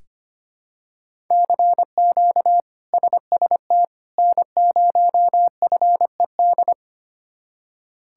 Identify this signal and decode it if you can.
Morse code, heard as a keyed tone, about 25 wpm, 700 Hz. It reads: CQ SST N0FED
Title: K1USN SST QSO Practice - 25wpm